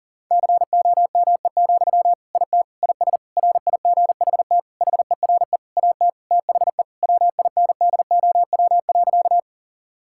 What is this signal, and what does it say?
Morse code, heard as a keyed tone, about 40 wpm, 700 Hz. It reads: COME, IT IS RIGHT HERE AT THE WINDOW.
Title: Sentences from Top 1600 Words 40wpm